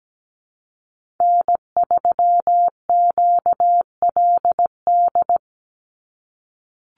Morse code, heard as a keyed tone, about 17 wpm, 700 Hz. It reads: N3QLD
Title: USA Call Signs 17wpm